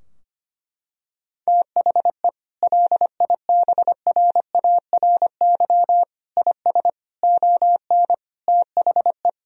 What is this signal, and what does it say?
Morse code, heard as a keyed tone, about 25 wpm, 700 Hz. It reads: THE LIBRARY IS ON THE
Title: Sentences from Top 1000 Words - Review Part 4 - 25wpm